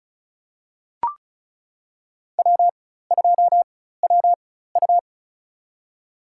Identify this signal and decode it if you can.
Morse code, heard as a keyed tone, about 36 wpm, spaced as for 16 wpm, 700 Hz. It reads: W2WU
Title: USA Call Signs - 4x Character Spacing - 35wpm